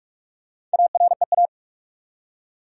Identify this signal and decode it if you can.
Morse code, heard as a keyed tone, about 46 wpm, 700 Hz. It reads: AREA
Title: Four Letter Words - 45wpm